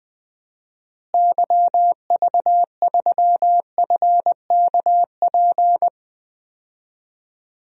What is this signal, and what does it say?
Morse code, heard as a keyed tone, about 20 wpm, 700 Hz. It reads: YV3FKP
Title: DXCC-100 Call Signs - 6 Characters - Repeated 2x - 20wpm